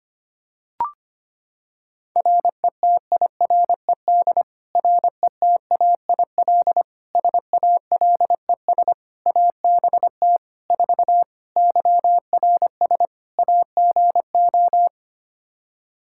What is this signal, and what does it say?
Morse code, heard as a keyed tone, about 25 wpm, 700 Hz. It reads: RETIRED RETAIL SALES ABT 4 YRS AGO
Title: Practice QSO - Exchange 3 - More Info - 25wpm